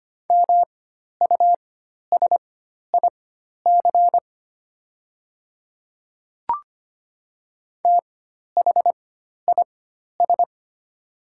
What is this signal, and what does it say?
Morse code, heard as a keyed tone, about 25 wpm, spaced as for 12 wpm, 700 Hz. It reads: MUSIC THIS
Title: 4x Character Spacing - Top 300 Words - 25wpm